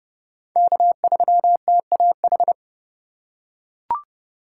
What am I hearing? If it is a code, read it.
Morse code, heard as a keyed tone, about 31 wpm, 700 Hz. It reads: K3TAH
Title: USA Call Signs - 5 Characters - 4x Character Spacing - 30wpm